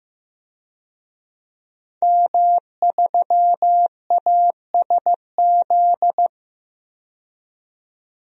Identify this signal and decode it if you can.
Morse code, heard as a keyed tone, about 15 wpm, 700 Hz. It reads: M3ASZ
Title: DXCC-100 Call Signs - 5 Characters - 15wpm